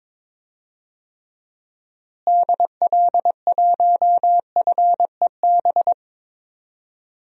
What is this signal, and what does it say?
Morse code, heard as a keyed tone, about 22 wpm, 700 Hz. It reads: DL1FEB
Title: DXCC-100 Call Signs - 6 Characters - Repeated 3x - 22wpm